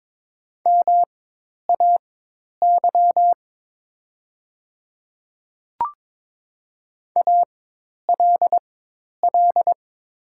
Morse code, heard as a keyed tone, about 22 wpm, spaced as for 10 wpm, 700 Hz. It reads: MAY ALL
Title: 4x Character Spacing - Top 100 Words - 22wpm